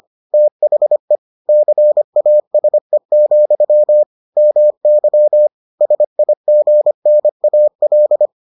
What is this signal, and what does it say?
Morse code, heard as a keyed tone, about 25 wpm, 600 Hz. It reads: THE CASE, MY SIGNAL